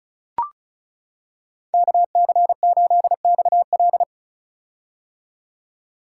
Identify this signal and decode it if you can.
Morse code, heard as a keyed tone, about 36 wpm, 700 Hz. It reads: KC8XL
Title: Code Groups - 5 Letter:Numbers - 35wpm